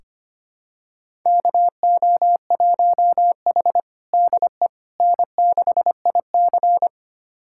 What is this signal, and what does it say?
Morse code, heard as a keyed tone, about 25 wpm, 700 Hz. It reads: KO1H DE N6IC
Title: USA Callsign DE Callsign - 4 Characters - 25wpm